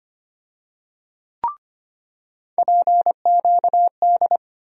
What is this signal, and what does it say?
Morse code, heard as a keyed tone, about 25 wpm, 700 Hz. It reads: PQD
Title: USA Call Sign Postfixes - 3 Letters 25wpm